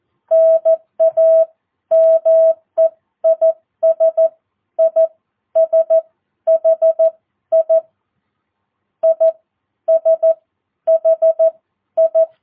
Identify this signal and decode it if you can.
Morse code, heard as a keyed tone, about 14 wpm, spaced as for 11 wpm, 660 Hz. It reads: NAME IS ISHI ISHI
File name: jh8cbh / 2022cw